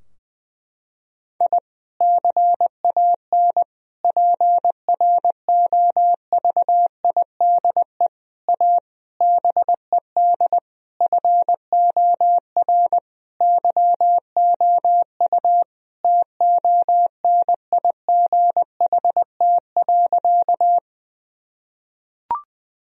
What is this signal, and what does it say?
Morse code, heard as a keyed tone, about 21 wpm, 700 Hz. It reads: I CAN PROVIDE A BED FOR YOU TONIGHT.
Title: Sentences from Top 1000 Words - Review Part 3 - 20wpm